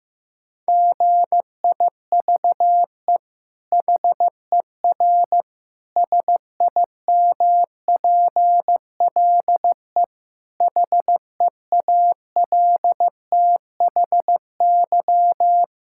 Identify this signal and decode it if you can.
Morse code, heard as a keyed tone, about 16 wpm, 700 Hz. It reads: GIVE HER SIMPLE HEALTHY
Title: Sentences from Top 1000 Words - Review Part 5 - 15wpm